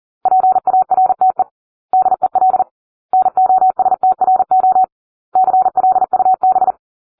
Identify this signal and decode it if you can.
Morse code, heard as a keyed tone, about 40 wpm, 750 Hz. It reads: PARTE DEL NOSTRO CLUB